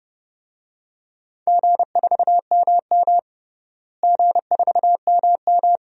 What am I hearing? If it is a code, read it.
Morse code, heard as a keyed tone, about 31 wpm, 700 Hz. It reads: G4MM G4MM
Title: DXCC-100 Call Signs - 4 Characters - Repeated 2x - 30wpm